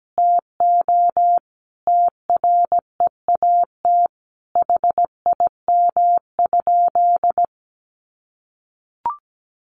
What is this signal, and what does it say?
Morse code, heard as a keyed tone, about 17 wpm, 700 Hz. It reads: TO TREAT HIM?
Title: Sentences from Top 1300 Words 17wpm